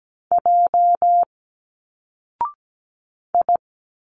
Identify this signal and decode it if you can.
Morse code, heard as a keyed tone, about 17 wpm, 700 Hz. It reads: J I
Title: Single Letters 17wpm